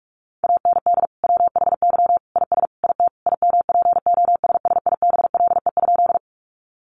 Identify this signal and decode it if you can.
Morse code, heard as a keyed tone, about 44 wpm, 700 Hz. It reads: AND WHY IS IT IMPOSSIBLE?
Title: Sentences from Top 1100 Words 45wpm